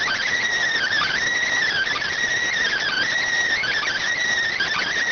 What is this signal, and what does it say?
Emergency Alert System (SAME) tones, heard as harsh buzzing bursts. It is heard from 0.7 to 4.9 s.